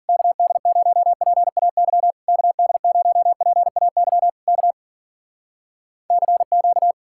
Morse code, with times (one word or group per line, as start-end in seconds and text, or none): 0.1-2.1 KD0PAY
2.3-4.3 KD0PAY
4.5-4.7 K
6.1-6.9 CQ